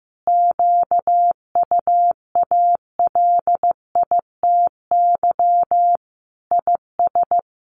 Morse code, heard as a keyed tone, about 15 wpm, 700 Hz. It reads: QUALITY IS
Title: Sentences from Top 1500 Words 15wpm